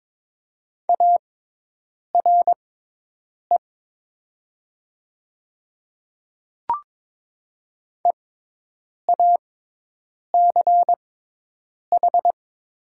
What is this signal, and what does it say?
Morse code, heard as a keyed tone, about 22 wpm, spaced as for 8 wpm, 700 Hz. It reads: ARE EACH
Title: 6x Character Spacing - Top 100 Words - 22wpm